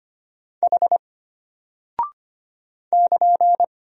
Morse code, heard as a keyed tone, about 25 wpm, 700 Hz. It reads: H <KN>
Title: Single Character (Letter:Number:Punctuation:Prosign) Alt 25wpm